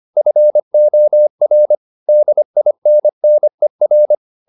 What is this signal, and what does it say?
Morse code, heard as a keyed tone, about 25 wpm, 600 Hz. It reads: FOR DINNER